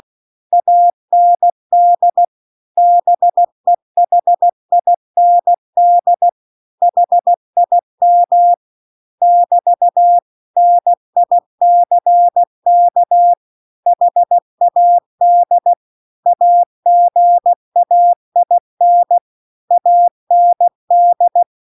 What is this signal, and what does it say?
Morse code, heard as a keyed tone, about 16 wpm, 700 Hz. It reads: AND BEHIND HIM <BT> NICK HAD AGAIN AND